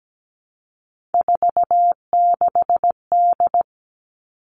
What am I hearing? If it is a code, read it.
Morse code, heard as a keyed tone, about 17 wpm, 700 Hz. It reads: 46D
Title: Three Challenging Letter-Number Combinations 17wpm